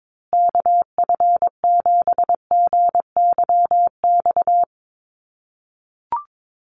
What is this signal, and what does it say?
Morse code, heard as a keyed tone, about 22 wpm, 700 Hz. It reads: KF7GYX
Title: USA Call Signs Repeated 3x 22wpm